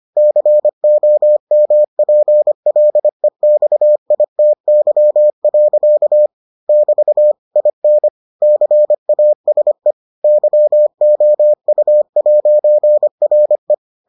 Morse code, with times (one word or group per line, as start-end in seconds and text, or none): 0.2-6.3 COMPLEXITY.
6.7-7.3 <BT>
7.6-8.1 IN
8.4-9.9 CASE
10.2-13.7 YOU'RE